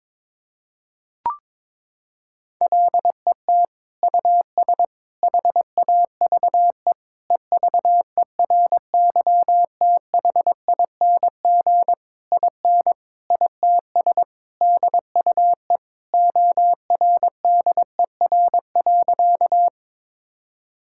Morse code, heard as a keyed tone, about 22 wpm, 700 Hz. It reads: LET US HAVE EVERYTHING IN ITS DUE ORDER.
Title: Sentences from Top 1000 Words - Review Part 6 - 22wpm